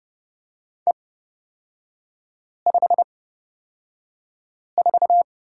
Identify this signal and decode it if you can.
Morse code, heard as a keyed tone, about 30 wpm, spaced as for 5 wpm, 700 Hz. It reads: E54
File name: Lesson 36e - BT - Three Characters 30wpm